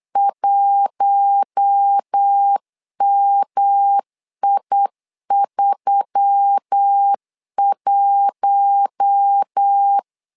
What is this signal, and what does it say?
Morse code, heard as a keyed tone, about 8 wpm, 800 Hz. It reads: 1MI31
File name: LEZIONI / 4 Q1S3 8wpm 3037